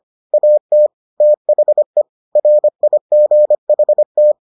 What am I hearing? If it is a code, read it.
Morse code, heard as a keyed tone, about 25 wpm, 600 Hz. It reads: AT THE RIGHT